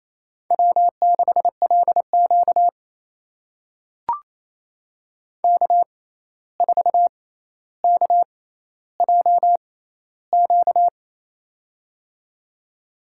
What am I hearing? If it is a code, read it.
Morse code, heard as a keyed tone, about 28 wpm, spaced as for 13 wpm, 700 Hz. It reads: W6LQ K4KJQ
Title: USA Call Signs - 6x Character Spacing - 28wpm